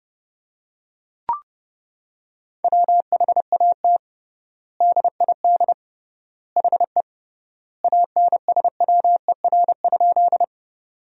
Morse code, heard as a keyed tone, about 30 wpm, 700 Hz. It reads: WHAT DID HE ANSWER?
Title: 3x Word-Spacing - Sentences from Top 300 Words - 30wpm